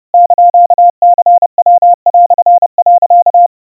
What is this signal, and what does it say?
Morse code (Japanese Wabun code), heard as a keyed tone, about 30 wpm, 700 Hz. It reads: <KN>ニヤ)、
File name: iroha 100 cpm 700 Hz Crispy